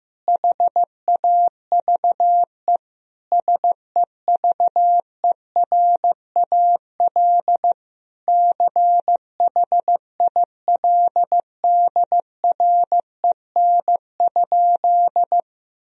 Morse code, heard as a keyed tone, about 15 wpm, 700 Hz. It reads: HAVE SEVERAL CHILDREN?
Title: Sentences from Top 1000 Words - Review Part 2 - 15wpm